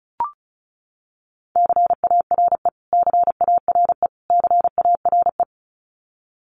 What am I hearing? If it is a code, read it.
Morse code, heard as a keyed tone, about 34 wpm, 700 Hz. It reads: CARE CARE CARE
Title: Top 300 Words Repeated 35wpm